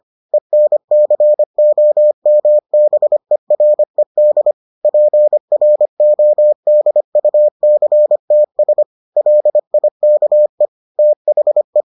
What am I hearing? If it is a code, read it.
Morse code, heard as a keyed tone, about 25 wpm, 600 Hz. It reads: ENCOMBERED PRODUCTS LIKE THE